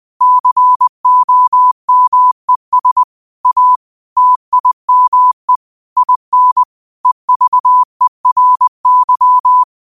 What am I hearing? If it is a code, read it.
Morse code, heard as a keyed tone, about 20 wpm, 1000 Hz. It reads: COMES A TIME IN EVERY